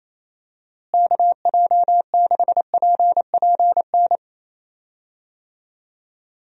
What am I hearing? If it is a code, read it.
Morse code, heard as a keyed tone, about 28 wpm, 700 Hz. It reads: KJ6PPN
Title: USA Call Signs - 6 Characters - Repeated 2x - 28wpm